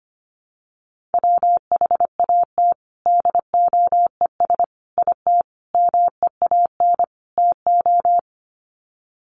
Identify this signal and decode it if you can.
Morse code, heard as a keyed tone, about 25 wpm, 700 Hz. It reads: WHAT DOES IT MEAN TO
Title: Sets of 5 Words Encore2 25wpm